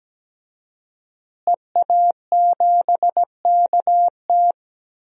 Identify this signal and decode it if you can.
Morse code, heard as a keyed tone, about 17 wpm, 700 Hz. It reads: EA7KT
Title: DXCC-100 Call Signs - 5 Characters - Repeated 3x - 17wpm